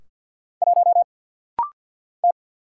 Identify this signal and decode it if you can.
Morse code, heard as a keyed tone, about 50 wpm, 700 Hz. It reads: 1 T
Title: Single Letter-Number - Rapid-Fire 50wpm